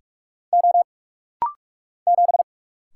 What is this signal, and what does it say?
Morse code, heard as a keyed tone, about 48 wpm, 700 Hz. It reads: O 7
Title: Single Letter-Number - Rapid-Fire 45wpm